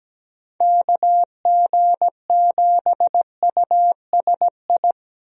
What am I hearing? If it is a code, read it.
Morse code, heard as a keyed tone, about 17 wpm, 700 Hz. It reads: KG7USI
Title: USA Call Signs - 6 Characters - 17wpm